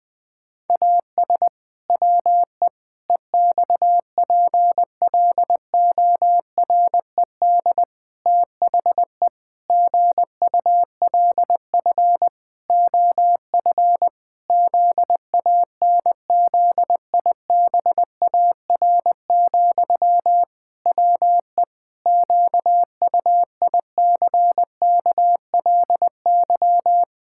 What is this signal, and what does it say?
Morse code, heard as a keyed tone, about 21 wpm, 700 Hz. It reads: AS WE EXPLORED THE GULF OF ZANZIBAR, WE QUICKLY
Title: Pangram Challenge v2 - 20wpm